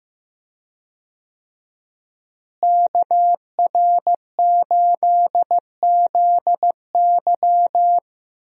Morse code, heard as a keyed tone, about 15 wpm, 700 Hz. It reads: KR8ZY
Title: USA Call Signs - 5 Characters - Repeated 3x - 15wpm